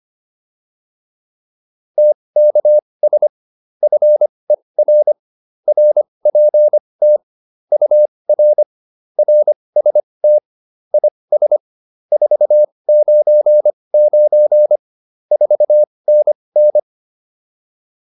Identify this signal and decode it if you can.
Morse code, heard as a keyed tone, about 25 wpm, spaced as for 20 wpm, 600 Hz. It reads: TKS FER RPT UR RST IS 499 4NN